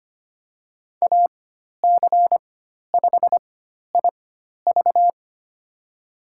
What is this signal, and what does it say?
Morse code, heard as a keyed tone, about 25 wpm, spaced as for 12 wpm, 700 Hz. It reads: AC5IV
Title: USA Call Signs - 5 Characters - 4x Character Spacing - 25wpm